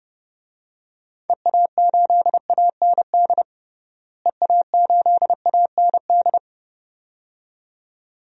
Morse code, heard as a keyed tone, about 30 wpm, 700 Hz. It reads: EA8AND EA8AND
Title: DXCC-100 Call Signs - 6 Characters - Repeated 2x - 30wpm